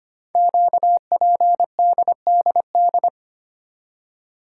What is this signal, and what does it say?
Morse code, heard as a keyed tone, about 25 wpm, 700 Hz. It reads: QPDDD
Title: Code Groups - 5 Letters - 25wpm